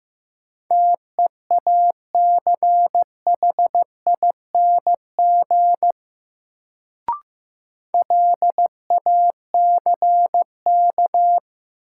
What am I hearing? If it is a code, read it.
Morse code, heard as a keyed tone, about 15 wpm, 700 Hz. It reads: TEACHING LACK
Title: Sentences from Top 1100 Words - New Word Drill - 15wpm